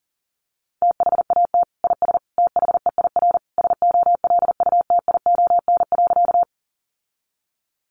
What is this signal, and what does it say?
Morse code, heard as a keyed tone, about 40 wpm, 700 Hz. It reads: THAT IS THEIR SOLUTION.
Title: Sentences from Top 1300 Words 40wpm